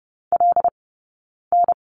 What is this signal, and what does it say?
Morse code, heard as a keyed tone, about 30 wpm, spaced as for 9 wpm, 690 Hz. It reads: LN
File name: course / 30:10wpm / 086 Lesson 15d - Number 5 - Two Characters 30wpm